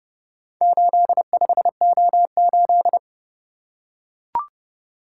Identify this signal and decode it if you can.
Morse code, heard as a keyed tone, about 30 wpm, 700 Hz. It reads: 85O8